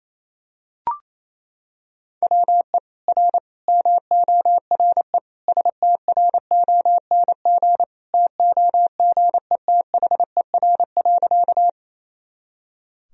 Morse code, heard as a keyed tone, about 28 wpm, 700 Hz. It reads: WE R MORE STRONG TOGETHER.
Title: Sentences from Top 500 Words and Abbreviations 28wpm